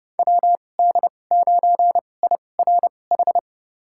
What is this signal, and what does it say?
Morse code, heard as a keyed tone, about 29 wpm, spaced as for 22 wpm, 700 Hz. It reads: WD9IRH
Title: USA Call Signs - 2x Character Spacing - 30wpm